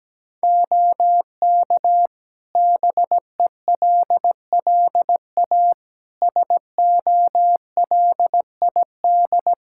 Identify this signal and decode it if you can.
Morse code, heard as a keyed tone, about 17 wpm, 700 Hz. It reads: OK BELLA SOLID
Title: K1USN SST QSO Practice 17wpm